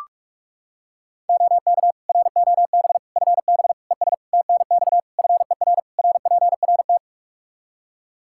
Morse code, heard as a keyed tone, about 43 wpm, 700 Hz. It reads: OK ROB FB ES TNX FER RPRT
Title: Practice QSO - Exchange 2 - Info - 45wpm